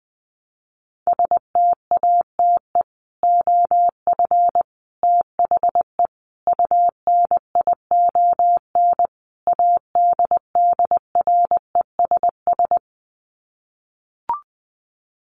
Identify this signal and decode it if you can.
Morse code, heard as a keyed tone, about 20 wpm, 700 Hz. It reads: STATE OF THE UNION ADDRESS
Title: Sets of 5 Words Encore 20wpm